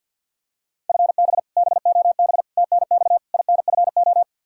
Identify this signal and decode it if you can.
Morse code, heard as a keyed tone, about 47 wpm, 700 Hz. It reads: FB BOB TNX INFO
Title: K1USN SST QSO Practice 50wpm